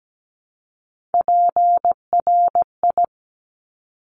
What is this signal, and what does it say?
Morse code, heard as a keyed tone, about 17 wpm, 700 Hz. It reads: PRI
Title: Top Three Letter Combinations in Words 17wpm